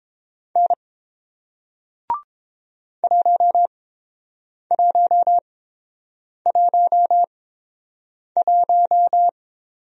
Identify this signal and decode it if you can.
Morse code, heard as a keyed tone, about 29 wpm, spaced as for 7 wpm, 700 Hz. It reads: N 1111